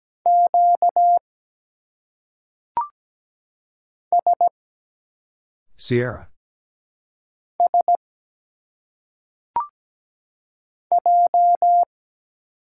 Morse code, heard as a keyed tone, about 17 wpm, 700 Hz. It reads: Q S S J
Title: Single Letters - Phonetic Pronunciation 17wpm